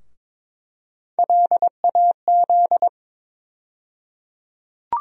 Morse code, heard as a keyed tone, about 22 wpm, 700 Hz. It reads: LAZ
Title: USA Call Sign Postfixes - 3 Letters 22wpm